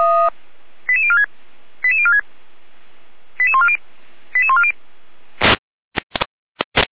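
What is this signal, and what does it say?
Five-tone selective call (ZVEI), heard as quick runs of five tones, tones of about 70 ms. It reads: 88046 88046 88140 88140
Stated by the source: Five-tone ZVEI selective call 88149, received on SDR.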